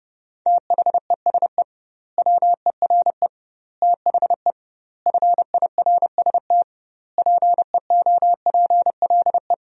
Morse code, heard as a keyed tone, about 30 wpm, 700 Hz. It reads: THESE WERE THE FIRST PEOPLE
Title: 2x Word-Spacing - Sentences from Top 300 Words - 30wpm